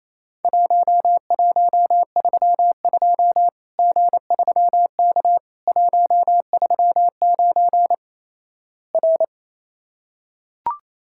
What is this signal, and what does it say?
Morse code, heard as a keyed tone, about 28 wpm, 700 Hz. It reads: 1132 G3K 139 R
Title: Worked All Europe DX Contest - QTC Exchange - 28wpm